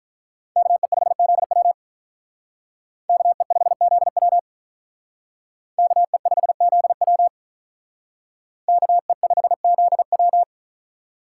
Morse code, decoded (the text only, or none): KE5ZW KE5ZW KE5ZW KE5ZW